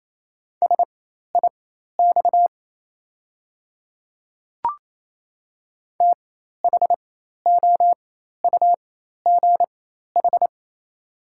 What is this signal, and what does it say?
Morse code, heard as a keyed tone, about 28 wpm, spaced as for 13 wpm, 700 Hz. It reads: SIX THOUGH
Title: 4x Character Spacing - Top 500 Words - 28wpm